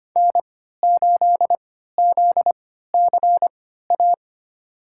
Morse code, heard as a keyed tone, about 25 wpm, spaced as for 14 wpm, 700 Hz. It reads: N8ZCA
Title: USA Call Signs - 5 Characters - 3x Character Spacing - 25wpm